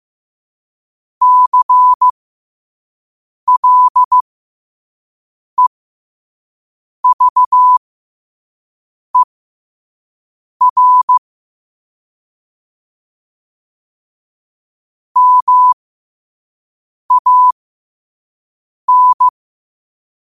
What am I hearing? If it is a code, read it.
Morse code, heard as a keyed tone, about 15 wpm, spaced as for 5 wpm, 1000 Hz. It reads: CLEVER MAN
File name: qotdfiles / 5wpm